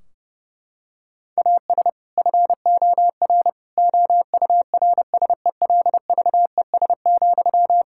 Morse code, heard as a keyed tone, about 30 wpm, 700 Hz. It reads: AS FOR OURSELVES,
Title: Sentences from Top 1600 Words 30wpm